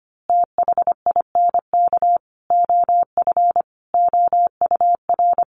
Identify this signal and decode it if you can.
Morse code, heard as a keyed tone, about 25 wpm, 700 Hz. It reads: THINK OF OUR
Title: Sentences from Top 1100 Words 25wpm